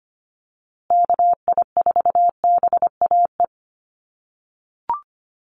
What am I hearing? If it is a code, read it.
Morse code, heard as a keyed tone, about 25 wpm, 700 Hz. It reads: KI4BAE
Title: USA Call Signs Repeated 2x 25wpm